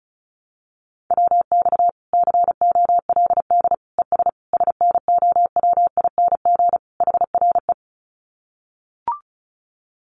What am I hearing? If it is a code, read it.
Morse code, heard as a keyed tone, about 34 wpm, 700 Hz. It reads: WX COLD ES SNOWING HRE
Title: Common QSO Phrases 35wpm